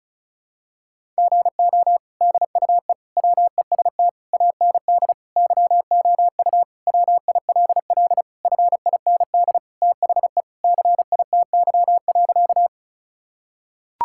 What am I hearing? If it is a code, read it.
Morse code, heard as a keyed tone, about 34 wpm, 700 Hz. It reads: GO DUE WEST AND YOU WILL FIND THE CITY.
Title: Sentences from Top 1000 Words - Review Part 6 - 35wpm